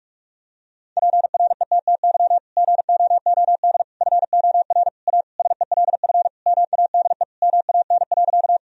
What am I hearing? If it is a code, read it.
Morse code, heard as a keyed tone, about 46 wpm, 700 Hz. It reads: PRETTY GOOD FOR A SELF MADE MAN.
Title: Sentences from Top 1000 Words - Review Part 1 - 45wpm